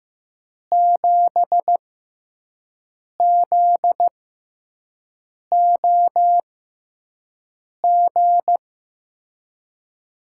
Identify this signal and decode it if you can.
Morse code, heard as a keyed tone, about 15 wpm, spaced as for 5 wpm, 700 Hz. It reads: 7ZOG
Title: USA Call Signs - 6 Characters - 6x Character Spacing - 15wpm